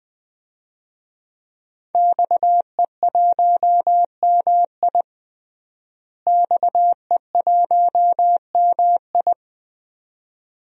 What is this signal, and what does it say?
Morse code, heard as a keyed tone, about 21 wpm, 700 Hz. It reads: XE1MI XE1MI
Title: DXCC-100 Call Signs - 5 Characters - Repeated 2x - 20wpm